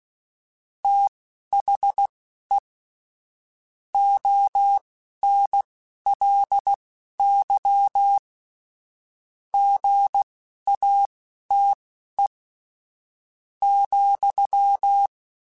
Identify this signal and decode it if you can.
Morse code, heard as a keyed tone, about 16 wpm, spaced as for 10 wpm, 800 Hz. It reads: THE ONLY GATE,
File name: Technical Subjects / Technical Subjects-mp3s / AND Gate @10